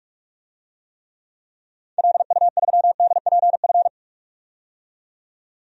Morse code, heard as a keyed tone, about 47 wpm, 700 Hz. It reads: LU3DPF
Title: DXCC-100 Call Signs - 6 Characters - 45wpm